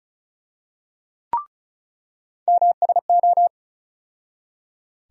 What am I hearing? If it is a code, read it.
Morse code, heard as a keyed tone, about 34 wpm, 700 Hz. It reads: MSO
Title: USA Call Sign Postfixes - 3 Letters 35wpm